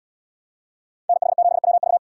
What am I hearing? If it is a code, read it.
Morse code, heard as a keyed tone, about 69 wpm, 700 Hz. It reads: NH7WV